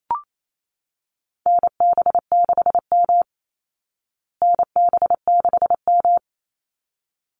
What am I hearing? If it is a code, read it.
Morse code, heard as a keyed tone, about 28 wpm, 700 Hz. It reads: NB6M NB6M
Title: Call Signs Repeated 3x - 28wpm